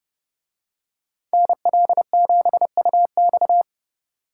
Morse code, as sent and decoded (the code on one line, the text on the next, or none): -. .-.. --... ..- -..-
NL7UX